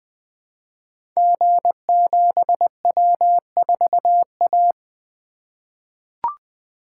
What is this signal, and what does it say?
Morse code, heard as a keyed tone, about 20 wpm, 700 Hz. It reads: G7W4A
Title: Code Groups - 5 Letter:Numbers - 20wpm